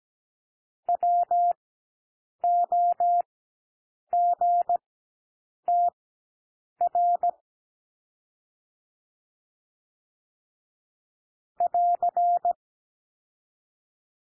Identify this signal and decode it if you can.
Morse code, heard as a keyed tone, about 17 wpm, spaced as for 7 wpm, 700 Hz. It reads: WOGTR <AR>
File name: SOUND LETTERS 17 wpm